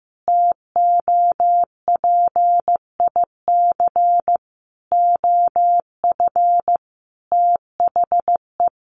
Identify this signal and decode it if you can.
Morse code, heard as a keyed tone, about 14 wpm, 700 Hz. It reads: TOPIC OF THE
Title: Sentences from Top 1200 Words 15wpm